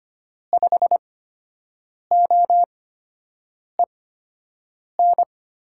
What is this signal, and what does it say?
Morse code, heard as a keyed tone, about 25 wpm, spaced as for 7 wpm, 700 Hz. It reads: 5OEN
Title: USA Call Signs - 6 Characters - 8x Character Spacing - 25wpm